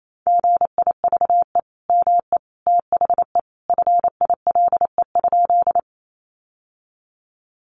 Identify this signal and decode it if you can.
Morse code, heard as a keyed tone, about 28 wpm, 700 Hz. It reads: GIVE ME THE FILE?
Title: Sentences from Top 1200 Words 28wpm